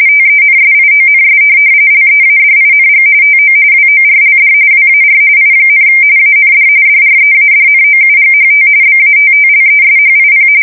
RTTY, about 45.45 baud. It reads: TENSIONS LEADING TO WORLD WAR I, AND DREADNOUGHT BATTLESHIP
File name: fsk170 / USS-New-Jersey